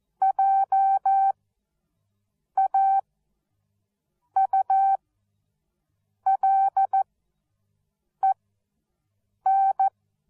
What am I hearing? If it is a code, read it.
Morse code, heard as a keyed tone, about 14 wpm, spaced as for 6 wpm, 780 Hz. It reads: JAULEN
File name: Onlinelehrgang / morsen / DJ4UF Morsen13